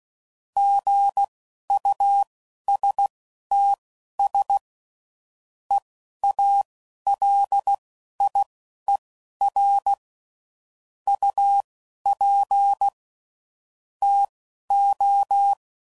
Morse code, heard as a keyed tone, about 16 wpm, spaced as for 11 wpm, 800 Hz. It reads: GUSTS EALIER UP TO
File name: CW Code Groups @10WPM / Weather Sets / Weather 5 5